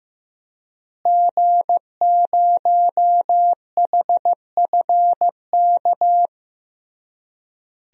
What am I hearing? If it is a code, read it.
Morse code, heard as a keyed tone, about 16 wpm, 700 Hz. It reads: G0HFK
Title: DXCC-100 Call Signs - 5 Characters - Repeated 3x - 15wpm